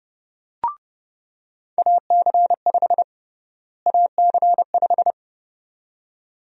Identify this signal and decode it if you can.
Morse code, heard as a keyed tone, about 30 wpm, 700 Hz. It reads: AC5 AC5
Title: USA Call Sign Prefixes - Repeated 2x - 30wpm